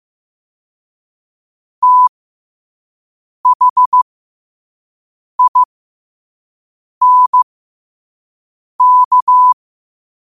Morse code, heard as a keyed tone, about 15 wpm, spaced as for 5 wpm, 1000 Hz. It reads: THINK
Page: New quotes every day in morse code at 5 Words per minute.